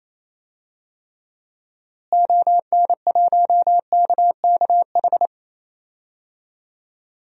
Morse code, heard as a keyed tone, about 28 wpm, 700 Hz. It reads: ON1KKH
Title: DXCC-100 Call Signs - 6 Characters - Repeated 2x - 28wpm